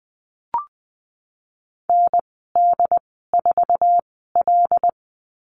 Morse code, heard as a keyed tone, about 20 wpm, spaced as for 15 wpm, 700 Hz. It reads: ND4L
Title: 076 Lesson 13g - Letter L - Call Signs 20wpm